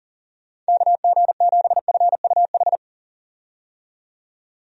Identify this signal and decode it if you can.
Morse code, heard as a keyed tone, about 40 wpm, 700 Hz. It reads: KG7FUH
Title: USA Call Signs - 6 Characters - 40wpm